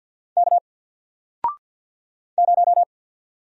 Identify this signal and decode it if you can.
Morse code, heard as a keyed tone, about 50 wpm, 700 Hz. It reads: K 0
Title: Single Letter-Number Alt 50wpm